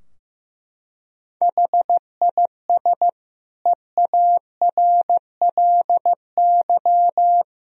Morse code, heard as a keyed tone, about 15 wpm, 700 Hz. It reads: HIS EARLY
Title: Sentences from Top 1000 Words - Review Part 4 - 15wpm